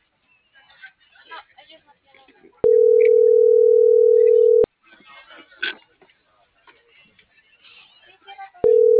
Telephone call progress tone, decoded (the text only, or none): ringback tone